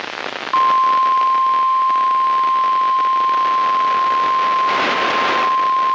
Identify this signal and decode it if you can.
Emergency Alert System (SAME) tones (the 1050 Hz attention tone), with noise. It is heard from 0.5 to 4.8 s.
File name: eas messages / EAS WXL38 Binghamton